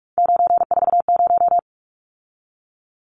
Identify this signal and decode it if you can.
Morse code, heard as a keyed tone, about 44 wpm, 700 Hz. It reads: 940
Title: Numbers - 3-Digit 45wpm